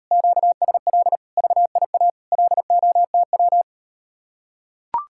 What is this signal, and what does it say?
Morse code, heard as a keyed tone, about 39 wpm, 700 Hz. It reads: QSL VIA LOTW